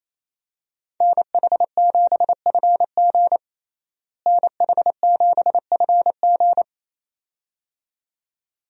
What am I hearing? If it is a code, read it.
Morse code, heard as a keyed tone, about 28 wpm, 700 Hz. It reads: NH7FG NH7FG
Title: USA Call Signs - 5 Characters - Repeated 2x - 28wpm